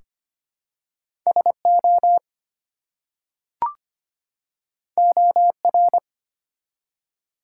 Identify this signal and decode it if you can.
Morse code, heard as a keyed tone, about 25 wpm, 700 Hz. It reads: SO OR
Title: Two Letter Words - 25wpm